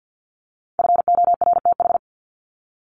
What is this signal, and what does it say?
Morse code, heard as a keyed tone, about 48 wpm, 700 Hz. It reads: FORTH
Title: Top 1751-2000 Words 50wpm